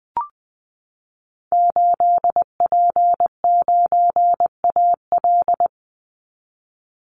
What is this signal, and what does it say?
Morse code, heard as a keyed tone, about 20 wpm, 700 Hz. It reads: 8P9AL
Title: Call Signs Repeated 2x - 20wpm